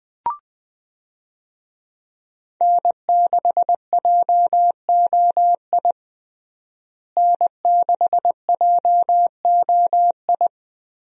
Morse code, heard as a keyed tone, about 20 wpm, 700 Hz. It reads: N6JOI N6JOI
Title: USA Call Signs - 5 Characters - Repeated 2x - 20wpm